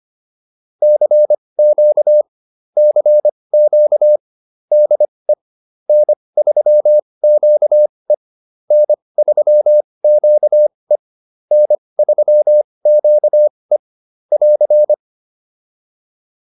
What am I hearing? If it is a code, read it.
Morse code, heard as a keyed tone, about 25 wpm, spaced as for 20 wpm, 600 Hz. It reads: CQ CQ DE N3QE N3QE N3QE <AR>
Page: advanced-practice-files